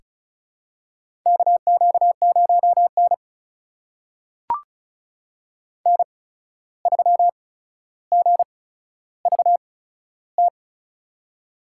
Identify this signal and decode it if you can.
Morse code, heard as a keyed tone, about 36 wpm, spaced as for 14 wpm, 700 Hz. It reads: KQ0N N3GVT
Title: USA Call Signs - 8x Character Spacing - 35wpm